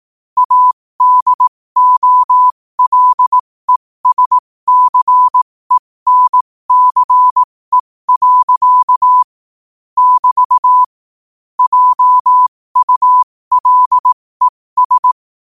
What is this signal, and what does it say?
Morse code, heard as a keyed tone, about 18 wpm, spaced as for 15 wpm, 1000 Hz. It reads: ADOLESCENCE. <BT> JULES
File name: qotdfiles / 15wpm